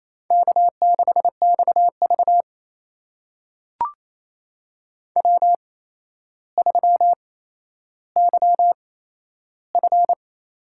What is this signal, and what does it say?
Morse code, heard as a keyed tone, about 28 wpm, spaced as for 12 wpm, 700 Hz. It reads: K6XV W3YF
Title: USA Call Signs - 4 Characters - 8x Character Spacing - 28wpm